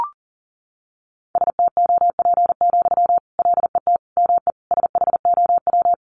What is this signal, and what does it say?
Morse code, heard as a keyed tone, about 40 wpm, 700 Hz. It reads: STOP, LET ME SHOW
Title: Sentences from Top 300 Words 40wpm